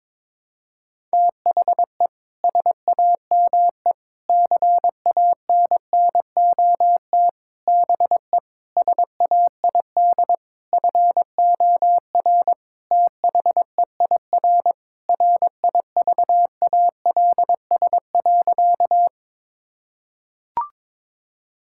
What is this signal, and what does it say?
Morse code, heard as a keyed tone, about 22 wpm, 700 Hz. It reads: THE SAME CANNOT BE SAID FOR THEIR RIVALS.
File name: Sentences with 7-Word Phrases - 22wpm